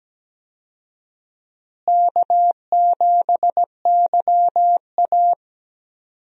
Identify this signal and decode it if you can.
Morse code, heard as a keyed tone, about 17 wpm, 700 Hz. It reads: K7YA
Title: USA Call Signs - 4 Characters - Repeated 2x - 17wpm